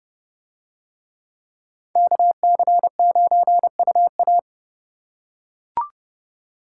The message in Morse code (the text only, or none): KC9UA